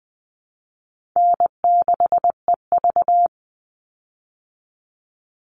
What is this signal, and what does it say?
Morse code, heard as a keyed tone, about 20 wpm, 700 Hz. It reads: N6EV
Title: Call Signs Repeated 3x - 20wpm